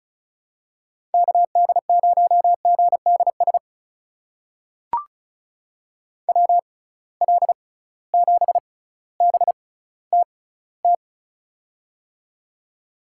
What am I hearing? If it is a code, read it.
Morse code, heard as a keyed tone, about 36 wpm, spaced as for 18 wpm, 700 Hz. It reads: KD0GDS WL7BTT
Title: USA Call Signs - 6 Characters - 6x Character Spacing - 35wpm